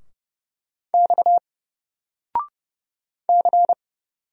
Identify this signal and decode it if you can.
Morse code, heard as a keyed tone, about 31 wpm, 700 Hz. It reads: X C
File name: Single-Letter-Number-30wpm